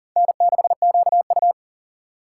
Morse code, heard as a keyed tone, about 40 wpm, 700 Hz. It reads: N6QU
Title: USA Call Signs - 4 Characters - 3x Character Spacing - 40wpm